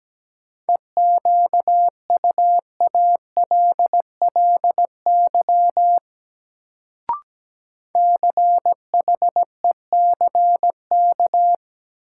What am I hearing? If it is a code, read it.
Morse code, heard as a keyed tone, about 17 wpm, 700 Hz. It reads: EQUALLY CHECK
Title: Sentences from Top 1400 Words - New Word Drill Alt - 17wpm